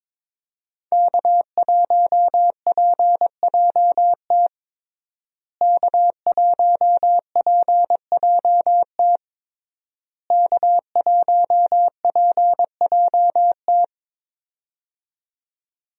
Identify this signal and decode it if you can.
Morse code, heard as a keyed tone, about 21 wpm, 700 Hz. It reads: K1PJT K1PJT K1PJT
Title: USA Call Signs - 5 Characters - Repeated 3x - 22wpm